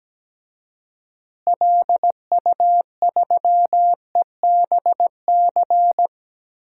Morse code, heard as a keyed tone, about 17 wpm, 700 Hz. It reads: LU3EBC
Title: DXCC-100 Call Signs - 6 Characters - Repeated 3x - 17wpm